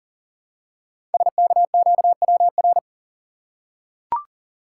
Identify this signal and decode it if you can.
Morse code, heard as a keyed tone, about 41 wpm, 700 Hz. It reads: SKQWR
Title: Code Groups - 5 Letter:Numbers - 40wpm